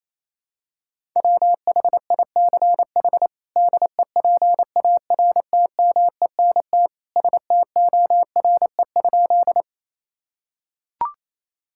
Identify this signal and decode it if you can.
Morse code, heard as a keyed tone, about 28 wpm, 700 Hz. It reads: WHICH DEPARTMENT STORE?
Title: Sentences from Top 1000 Words - Review Part 5 - 28wpm